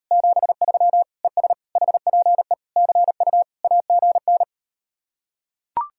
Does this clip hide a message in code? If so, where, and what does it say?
Yes: Morse code from 0.1 to 4.4 s, heard as a keyed tone, about 38 wpm, 700 Hz. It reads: 73 ES HPE CU AGN